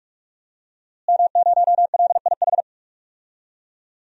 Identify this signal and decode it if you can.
Morse code, heard as a keyed tone, about 46 wpm, 700 Hz. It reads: M0LIH
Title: DXCC-100 Call Signs - 5 Characters - Repeated 2x - 45wpm